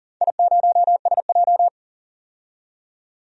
Morse code, heard as a keyed tone, about 38 wpm, 700 Hz. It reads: I0SJ
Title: DXCC-100 Call Signs - 4 Characters - 40wpm